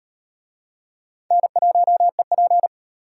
Morse code, heard as a keyed tone, about 37 wpm, 700 Hz. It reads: N1EP